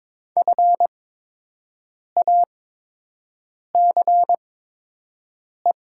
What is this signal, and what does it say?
Morse code, heard as a keyed tone, about 22 wpm, spaced as for 6 wpm, 700 Hz. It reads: FACE
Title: 8x Character Spacing - Top 500 Words - 22wpm